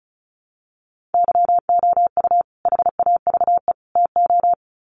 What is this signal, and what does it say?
Morse code, heard as a keyed tone, about 34 wpm, 700 Hz. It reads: YOU HAVE TO
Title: Sets of 3 Words 35wpm